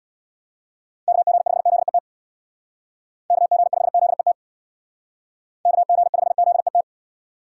Morse code, heard as a keyed tone, about 63 wpm, 700 Hz. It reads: KG5ZET KG5ZET KG5ZET